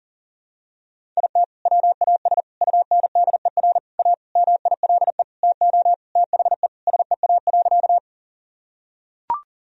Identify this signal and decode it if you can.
Morse code, heard as a keyed tone, about 40 wpm, 700 Hz. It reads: IT WAS UNDER A MILE TO THE SEA.
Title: Sentences from Top 1000 Words - Review Part 1 - 40wpm